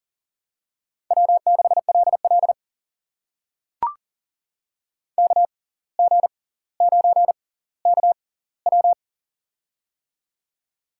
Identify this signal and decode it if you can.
Morse code, heard as a keyed tone, about 40 wpm, spaced as for 19 wpm, 700 Hz. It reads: W6LL KG9KW
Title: USA Call Signs - 6x Character Spacing - 40wpm